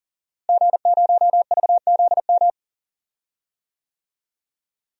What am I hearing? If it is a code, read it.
Morse code, heard as a keyed tone, about 40 wpm, 700 Hz. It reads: G0VZM
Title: DXCC-100 Call Signs - 5 Characters - Repeated 3x - 40wpm